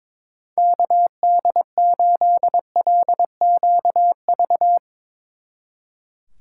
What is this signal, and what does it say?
Morse code, heard as a keyed tone, about 24 wpm, 700 Hz. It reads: KD8LQV
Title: Speed Racing - USA 2x3 Call Signs 22wpm